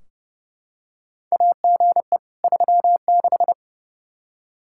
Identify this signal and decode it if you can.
Morse code, heard as a keyed tone, about 30 wpm, 700 Hz. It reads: AGE 36
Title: Practice QSO - Exchange 3 - More Info - 30wpm